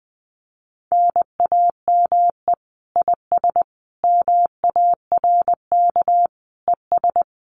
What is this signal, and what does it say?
Morse code, heard as a keyed tone, about 20 wpm, 700 Hz. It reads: NAME IS MARK ES
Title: Common QSO Phrases Encore 20wpm